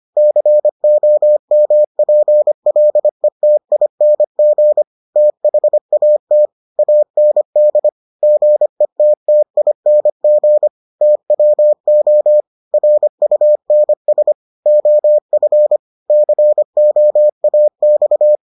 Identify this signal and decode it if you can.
Morse code, heard as a keyed tone, about 25 wpm, 600 Hz. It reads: COMPLETING THAT AND GETTING TWO RUNS OF COAX